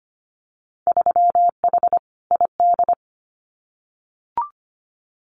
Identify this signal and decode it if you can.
Morse code, heard as a keyed tone, about 25 wpm, 700 Hz. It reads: 3H ID
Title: Winter Field Day Exchanges Encore 25wpm